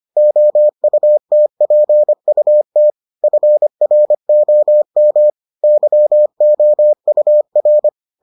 Morse code, heard as a keyed tone, about 25 wpm, 600 Hz. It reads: OUTPUT FROM YOUR